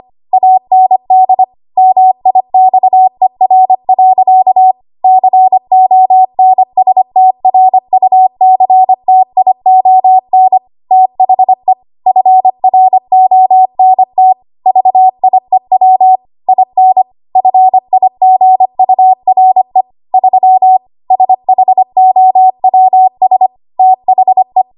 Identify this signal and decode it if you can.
Morse code, heard as a keyed tone, about 25 wpm, 750 Hz. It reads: AND MIXER. CONSTRUCTION THE FRONT VIEW IN FIGURE 3 SHOWS THE
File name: Archive / 25 WPM / 25WPM